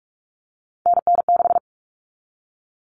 Morse code, heard as a keyed tone, about 44 wpm, 700 Hz. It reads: NN6
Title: All Cut Numbers - 3-Digit 45wpm